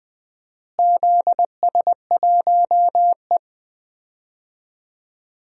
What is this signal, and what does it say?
Morse code, heard as a keyed tone, about 21 wpm, 700 Hz. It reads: ZS1E
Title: DXCC-100 Call Signs - 4 Characters - Repeated 3x - 20wpm